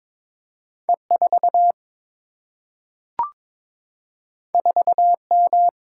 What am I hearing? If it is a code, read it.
Morse code, heard as a keyed tone, about 22 wpm, 700 Hz. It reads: E4 4M
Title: Single-Letter Single-Number 22wpm